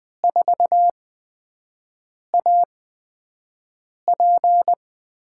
Morse code, heard as a keyed tone, about 20 wpm, spaced as for 5 wpm, 700 Hz. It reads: 4AP
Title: USA Call Signs - 5 Characters - 8x Character Spacing - 20wpm